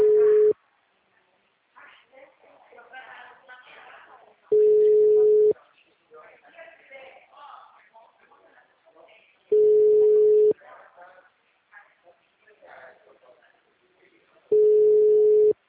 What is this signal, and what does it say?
Ringback tone (Europe, 425 Hz), with speech.